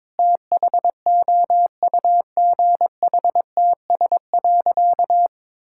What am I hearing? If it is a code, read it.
Morse code, heard as a keyed tone, about 21 wpm, 700 Hz. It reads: THOUGHTS.
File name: Difficult Sentences with Hints - 22wpm